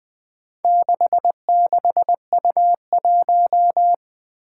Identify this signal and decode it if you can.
Morse code, heard as a keyed tone, about 20 wpm, 700 Hz. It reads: 66U1
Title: Four Challenging Letter-Number Combinations 20wpm